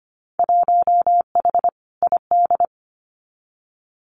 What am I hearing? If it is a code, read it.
Morse code, heard as a keyed tone, about 25 wpm, 700 Hz. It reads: 1H ID
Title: Winter Field Day Exchanges 25wpm